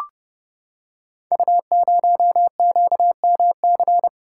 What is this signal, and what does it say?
Morse code, heard as a keyed tone, about 30 wpm, 700 Hz. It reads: U0QMC
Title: Code Groups - 5 Letter:Numbers - 30wpm